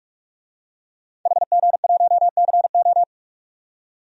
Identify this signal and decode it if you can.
Morse code, heard as a keyed tone, about 44 wpm, 700 Hz. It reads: HG1CO
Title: DXCC-100 Call Signs - 5 Characters - 45wpm